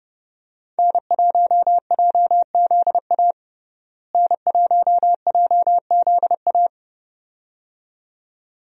Morse code, heard as a keyed tone, about 29 wpm, 700 Hz. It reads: N1JZA N1JZA
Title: USA Call Signs - 5 Characters - Repeated 2x - 30wpm